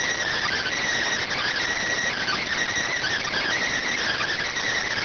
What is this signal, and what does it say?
Emergency Alert System (SAME) tones, heard as harsh buzzing bursts. It is heard from 0.4 to 4.2 s.